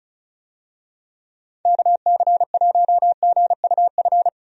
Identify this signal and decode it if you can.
Morse code, heard as a keyed tone, about 34 wpm, 700 Hz. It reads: KC1GUF
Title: USA Call Signs - 6 Characters - 35wpm